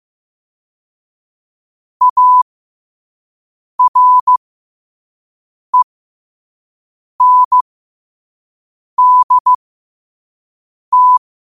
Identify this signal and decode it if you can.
Morse code, heard as a keyed tone, about 15 wpm, spaced as for 5 wpm, 1000 Hz. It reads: ARENDT
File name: qotdfiles / 5wpm